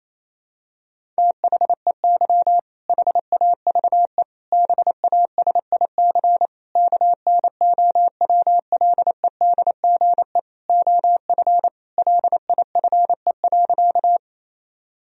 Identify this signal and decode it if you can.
Morse code, heard as a keyed tone, about 28 wpm, 700 Hz. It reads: THEY HAVE BASIC KNOWLEDGE OF LIFE.
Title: Sentences from Top 1000 Words - Review Part 3 - 28wpm